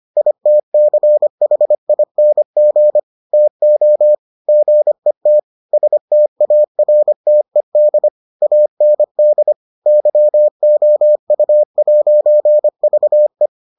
Morse code, heard as a keyed tone, about 25 wpm, 600 Hz. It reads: ITCHING TO GET STARTED AND YOU'VE